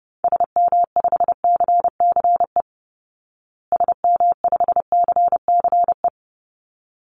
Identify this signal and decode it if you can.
Morse code, heard as a keyed tone, about 28 wpm, 700 Hz. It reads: SM5CCE SM5CCE
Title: Call Signs Repeated 3x - 30wpm